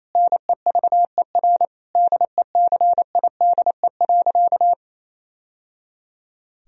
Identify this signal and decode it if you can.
Morse code, heard as a keyed tone, about 28 wpm, 700 Hz. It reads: NEVER DECIDE.
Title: Sentences from Top 1000 Words - Review Part 3 - 28wpm